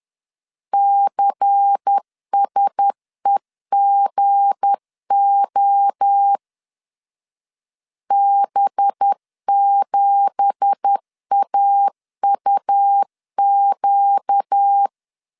Morse code, heard as a keyed tone, about 10 wpm, 800 Hz. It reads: CSEGO B7AUQ
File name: CW / DET / 1 9 1500C S B 10wpm 4439